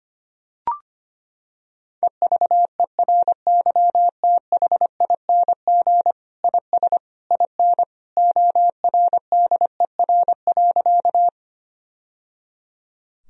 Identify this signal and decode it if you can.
Morse code, heard as a keyed tone, about 25 wpm, 700 Hz. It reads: EVERYTHING IS IN ORDER.
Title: Sentences from Top 1000 Words - Review Part 2 - 25wpm